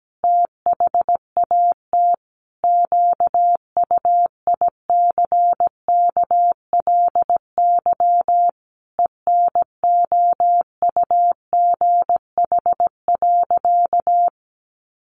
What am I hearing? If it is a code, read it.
Morse code, heard as a keyed tone, about 17 wpm, 700 Hz. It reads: THAT QUICKLY ENOUGH.
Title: Sentences from Top 700 Words 17wpm